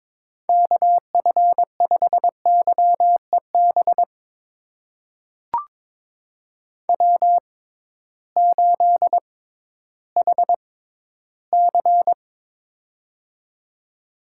Morse code, heard as a keyed tone, about 22 wpm, spaced as for 13 wpm, 700 Hz. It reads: KF5YEB W8HC
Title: USA Call Signs - 6x Character Spacing - 22wpm